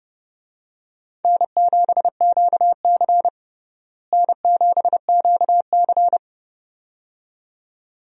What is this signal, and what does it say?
Morse code, heard as a keyed tone, about 30 wpm, 700 Hz. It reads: N7QC N7QC
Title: USA Call Signs - 4 Characters - Repeated 2x - 30wpm